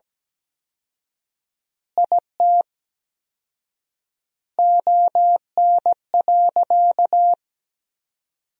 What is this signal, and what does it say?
Morse code, heard as a keyed tone, about 17 wpm, 700 Hz. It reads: IT ON.
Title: 4x Word-Spacing - Sentences from Top 200 Words - 17wpm